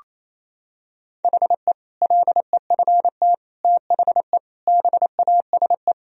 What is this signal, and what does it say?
Morse code, heard as a keyed tone, about 28 wpm, 700 Hz. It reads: HE LEFT THE BASE
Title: Sentences from Top 1000 Words - Review Part 1 - 28wpm